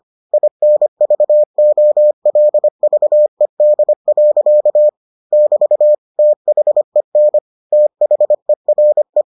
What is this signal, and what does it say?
Morse code, heard as a keyed tone, about 25 wpm, 600 Hz. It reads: INVOLVED. <BT> THEN THERE